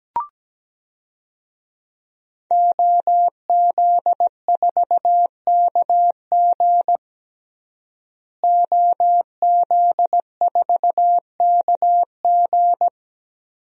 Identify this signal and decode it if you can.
Morse code, heard as a keyed tone, about 17 wpm, 700 Hz. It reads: OZ4KG OZ4KG
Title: DXCC-100 Call Signs - 5 Characters - Repeated 3x - 17wpm